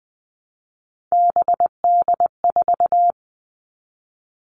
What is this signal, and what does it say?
Morse code, heard as a keyed tone, about 20 wpm, 700 Hz. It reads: BD4
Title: Callsign Prefixes for Top 100 DXCC - 20wpm